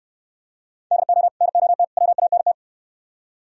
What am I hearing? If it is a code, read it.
Morse code, heard as a keyed tone, about 72 wpm, 700 Hz. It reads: DO NOT WANT